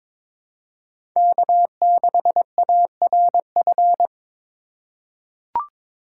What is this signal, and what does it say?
Morse code, heard as a keyed tone, about 22 wpm, 700 Hz. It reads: K6ARF
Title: USA Call Signs - 5 Characters - 2x Character Spacing - 22wpm